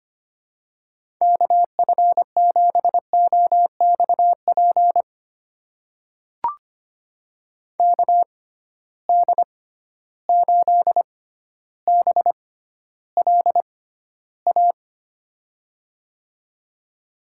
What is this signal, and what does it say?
Morse code, heard as a keyed tone, about 25 wpm, spaced as for 13 wpm, 700 Hz. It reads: KF7OXP KD8BLA
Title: USA Call Signs - 6 Characters - 6x Character Spacing - 25wpm